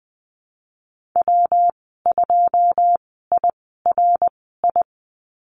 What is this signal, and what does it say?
Morse code, heard as a keyed tone, about 19 wpm, spaced as for 14 wpm, 700 Hz. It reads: W2IRI
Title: 122 Lesson 20g - Number 3 - Call Signs 20wpm